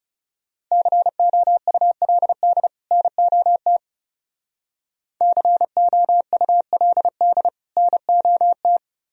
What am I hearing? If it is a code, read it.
Morse code, heard as a keyed tone, about 36 wpm, 700 Hz. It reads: COULD NOT COULD NOT
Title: Speed-Racing - Sets of 2 Words 30wpm